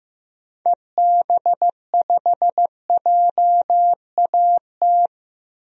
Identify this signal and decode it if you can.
Morse code, heard as a keyed tone, about 16 wpm, 700 Hz. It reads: EB5JAT
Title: DXCC-100 Call Signs - 6 Characters - 15wpm